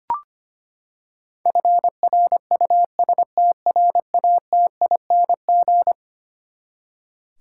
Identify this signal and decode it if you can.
Morse code, heard as a keyed tone, about 25 wpm, 700 Hz. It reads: FRUSTRATING
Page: morse-code-ninja-practice-25wpm